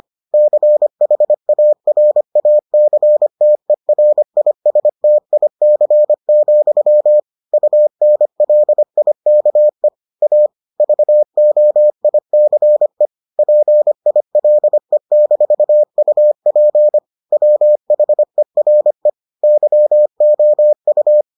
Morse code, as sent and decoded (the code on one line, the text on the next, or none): -.-. .... .- .-. .- -.-. - . .-. .. ... - .. -.-. --..-- / ..- -. .-.. .. -.- . / .- / ...- --- .. -.-. / .--. .. .-.. -....- ..- .--. / .-- .... .-. / -.-- --- ..-
CHARACTERISTIC, UNLIKE A VOICE PILE-UP WHERE YOU